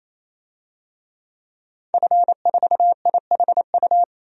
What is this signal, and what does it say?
Morse code, heard as a keyed tone, about 28 wpm, 700 Hz. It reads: F4IHU
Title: DXCC-100 Call Signs - 5 Characters - 28wpm